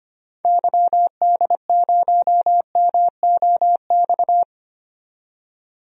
Morse code, heard as a keyed tone, about 25 wpm, 700 Hz. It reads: YD0MOX
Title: DXCC-100 Call Signs - 6 Characters - 25wpm